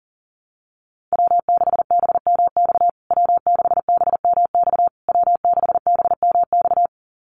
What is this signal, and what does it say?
Morse code, heard as a keyed tone, about 40 wpm, 700 Hz. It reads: W6BMX W6BMX W6BMX
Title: USA Call Signs Repeated 3x 40wpm